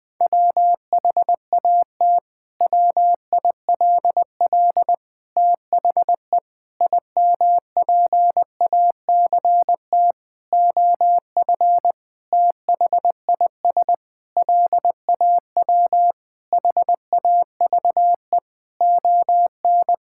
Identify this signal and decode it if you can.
Morse code, heard as a keyed tone, about 20 wpm, 700 Hz. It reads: WHAT WILL THE IMPACT OF THIS LAW HAVE ON
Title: Sentences from Top 1000 Words - Review Part 6 - 20wpm